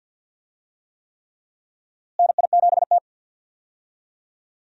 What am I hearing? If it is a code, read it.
Morse code, heard as a keyed tone, about 50 wpm, 700 Hz. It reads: NI7T
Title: USA Call Signs - 4 Characters - Repeated 3x - 50wpm